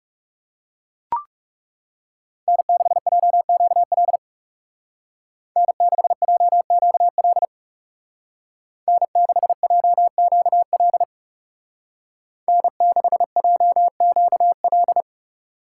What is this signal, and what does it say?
Morse code, heard as a keyed tone, about 40 wpm, 700 Hz. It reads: N6JQL N6JQL N6JQL N6JQL